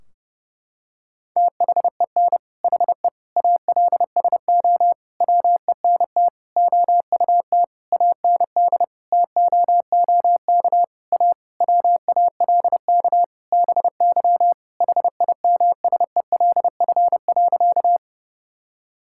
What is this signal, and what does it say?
Morse code, heard as a keyed tone, about 30 wpm, 700 Hz. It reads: THEN HE ALSO WENT OUT AND TOOK A WALK BY HIMSELF.
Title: Sentences from Top 1600 Words 30wpm